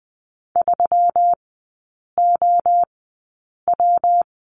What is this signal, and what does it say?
Morse code, heard as a keyed tone, about 20 wpm, spaced as for 8 wpm, 700 Hz. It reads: 3OW
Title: 152 Lesson 25e - Letter p - Three Characters 20wpm